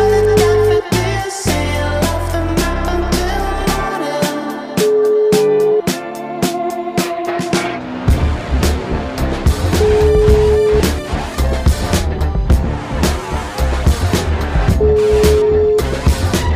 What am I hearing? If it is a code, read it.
Ringback tone (Europe, 425 Hz), with noise.